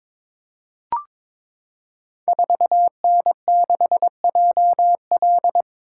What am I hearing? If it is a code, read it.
Morse code, heard as a keyed tone, about 22 wpm, 700 Hz. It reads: 4N6JL
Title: Code Groups - 5 Letter:Numbers - 22wpm